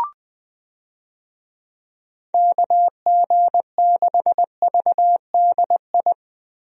Morse code, heard as a keyed tone, about 21 wpm, 700 Hz. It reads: KG6VDI
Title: USA Call Signs - 6 Characters - Repeated 3x - 20wpm